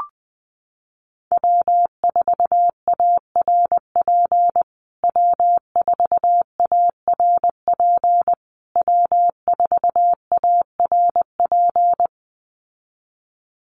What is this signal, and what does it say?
Morse code, heard as a keyed tone, about 20 wpm, 700 Hz. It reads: W4ARP W4ARP W4ARP
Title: USA Call Signs Repeated 3x 20wpm